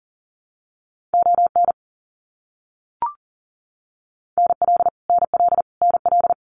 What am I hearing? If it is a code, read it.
Morse code, heard as a keyed tone, about 40 wpm, 700 Hz. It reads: ON NL NL NL
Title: Canadian Provinces and Territories - Repeated 3x - 40wpm